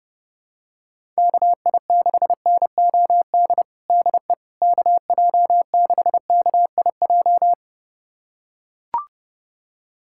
Morse code, heard as a keyed tone, about 32 wpm, 700 Hz. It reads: KI6NOD DE KJ6KIJ
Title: USA Callsign DE Callsign - 6 Characters - 30wpm